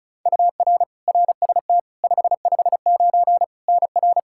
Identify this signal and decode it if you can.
Morse code, heard as a keyed tone, about 36 wpm, 700 Hz. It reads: UR RST 559 NR
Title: K1USN SST QSO Practice 35wpm